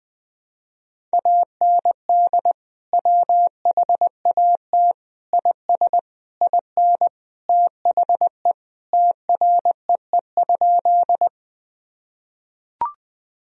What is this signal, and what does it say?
Morse code, heard as a keyed tone, about 21 wpm, 700 Hz. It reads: AND WHAT IS IN THE TREE?
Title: Sentences from Top 1000 Words - Review Part 1 - 20wpm